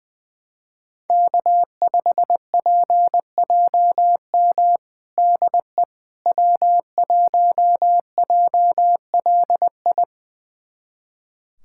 Morse code, heard as a keyed tone, about 20 wpm, 700 Hz. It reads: K5PJM DE W1JLI
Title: USA Callsign DE Callsign - 5 Characters - 20wpm